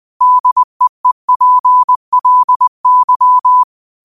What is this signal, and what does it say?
Morse code, heard as a keyed tone, about 20 wpm, 1000 Hz. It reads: DEEPLY
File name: qotdfiles / 20wpm